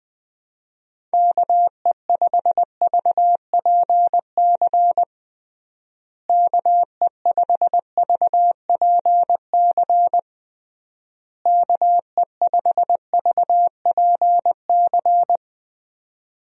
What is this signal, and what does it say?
Morse code, heard as a keyed tone, about 20 wpm, 700 Hz. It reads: KE5VPC KE5VPC KE5VPC
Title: USA Call Signs - 6 Characters - Repeated 3x - 20wpm